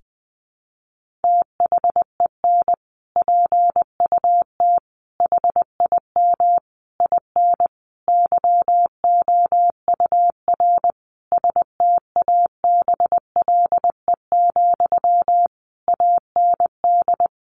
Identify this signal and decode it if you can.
Morse code, heard as a keyed tone, about 20 wpm, 700 Hz. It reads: THEN PUT HIM IN YOUR STABLE, AND
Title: Sentences from Top 1500 Words 20wpm